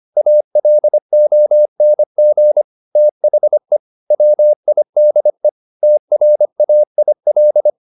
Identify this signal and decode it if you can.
Morse code, heard as a keyed tone, about 25 wpm, 600 Hz. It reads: ALONG THE WIDE TRAIL